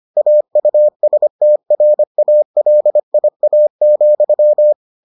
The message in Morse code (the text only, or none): AUSTRALIA,